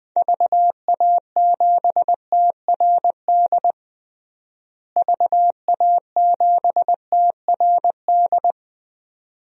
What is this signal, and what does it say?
Morse code, heard as a keyed tone, about 21 wpm, 700 Hz. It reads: VA7TRD VA7TRD
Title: DXCC-100 Call Signs - 6 Characters - Repeated 2x - 20wpm